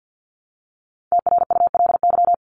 Morse code, heard as a keyed tone, about 50 wpm, 700 Hz. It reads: TRULY
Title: Top 1501-1750 Words 50wpm